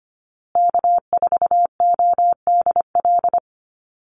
Morse code, heard as a keyed tone, about 25 wpm, 700 Hz. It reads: K4ODL
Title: Call Signs Repeated 2x - 25wpm